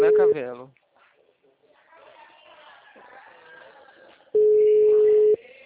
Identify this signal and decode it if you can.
Ringback tone (Europe, 425 Hz), with speech.